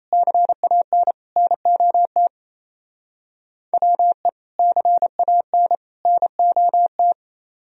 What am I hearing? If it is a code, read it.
Morse code, heard as a keyed tone, about 33 wpm, 700 Hz. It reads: CAN NOT WE CAN NOT
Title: Speed-Racing - Sets of 3 Words 28wpm